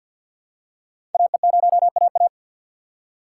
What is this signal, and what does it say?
Morse code, heard as a keyed tone, about 51 wpm, 700 Hz. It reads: AE0AA
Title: USA Call Signs - 5 Characters - 2x Character Spacing - 50wpm